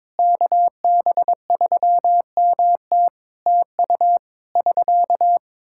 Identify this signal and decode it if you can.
Morse code, heard as a keyed tone, about 22 wpm, 700 Hz. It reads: KB3MT TU <SK>
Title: Practice QSO - Exchange 4 - Ending - 22wpm